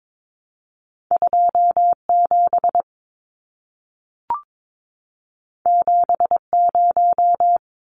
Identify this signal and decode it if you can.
Morse code, heard as a keyed tone, about 22 wpm, 700 Hz. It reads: 27 70
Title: Numbers - 2-Digit 22wpm